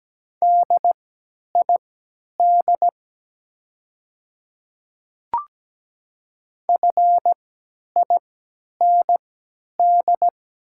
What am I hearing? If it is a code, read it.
Morse code, heard as a keyed tone, about 17 wpm, spaced as for 10 wpm, 700 Hz. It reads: DID FIND
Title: 3x Character Spacing - Top 100 Words - 17wpm